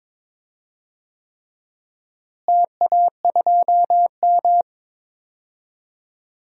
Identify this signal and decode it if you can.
Morse code, heard as a keyed tone, about 22 wpm, 700 Hz. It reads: TA2M
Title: DXCC-100 Call Signs - 4 Characters - Repeated 3x - 22wpm